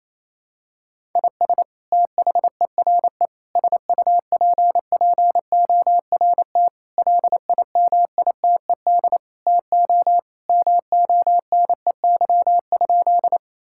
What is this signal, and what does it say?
Morse code, heard as a keyed tone, about 28 wpm, 700 Hz. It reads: IS THERE SUPPORT LIMITED TO MONEY?
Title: Sentences from Top 1000 Words - Review Part 6 - 28wpm